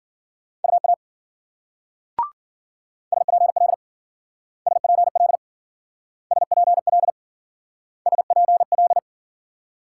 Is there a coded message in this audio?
Yes: Morse code from 0.6 to 9.0 s, heard as a keyed tone, about 51 wpm, 700 Hz. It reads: UA SPL SPL SPL SPL